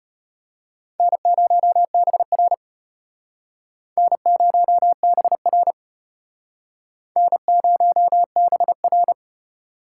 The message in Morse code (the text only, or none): N0BR N0BR N0BR